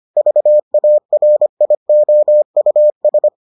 Morse code, heard as a keyed tone, about 25 wpm, 600 Hz. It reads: VARIOUS